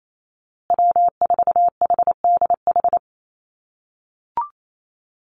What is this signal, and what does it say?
Morse code, heard as a keyed tone, about 28 wpm, 700 Hz. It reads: W4HDH
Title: USA Call Signs Repeated 2x 28wpm